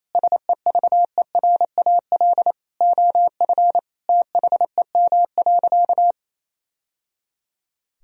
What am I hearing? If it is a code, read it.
Morse code, heard as a keyed tone, about 28 wpm, 700 Hz. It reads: SEVERAL OF THEM.
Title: Sentences from Top 1000 Words - Review Part 2 - 28wpm